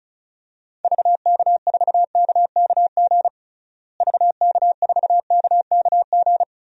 Morse code, heard as a keyed tone, about 34 wpm, 700 Hz. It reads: VK4KKG VK4KKG
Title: DXCC-100 Call Signs - 6 Characters - Repeated 2x - 35wpm